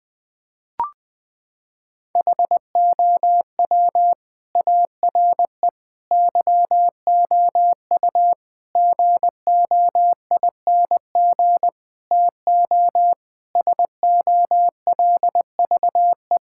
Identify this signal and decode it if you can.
Morse code, heard as a keyed tone, about 21 wpm, 700 Hz. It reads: HOW ARE YOU GOING TO SOLVE
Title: Sentences with 5-Word Phrases - 20wpm